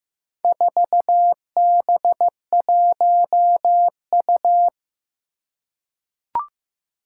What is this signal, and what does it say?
Morse code, heard as a keyed tone, about 15 wpm, 700 Hz. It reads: 4B1U
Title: Four Challenging Letter-Number Combinations 15wpm